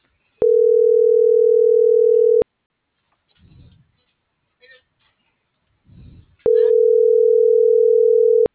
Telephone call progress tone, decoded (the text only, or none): ringback tone